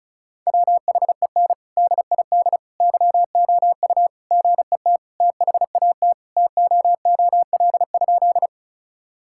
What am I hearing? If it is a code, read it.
Morse code, heard as a keyed tone, about 36 wpm, 700 Hz. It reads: WHEN DID YOU GET THAT TOOL?
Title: Sentences from Top 1000 Words - Review Part 6 - 35wpm